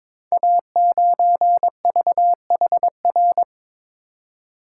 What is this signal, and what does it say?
Morse code, heard as a keyed tone, about 23 wpm, 700 Hz. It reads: A9VHR
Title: Code Groups - 5 Letter:Numbers - 22wpm